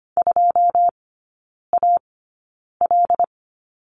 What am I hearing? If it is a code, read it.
Morse code, heard as a keyed tone, about 25 wpm, spaced as for 9 wpm, 690 Hz. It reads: 2AL
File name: Lesson 14e - Number 2 - Three Characters 25wpm